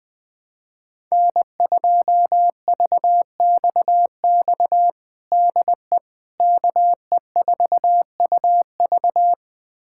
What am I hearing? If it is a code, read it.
Morse code, heard as a keyed tone, about 20 wpm, 700 Hz. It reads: N2VXX DE KE4UV
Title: USA Callsign DE Callsign - 5 Characters - 20wpm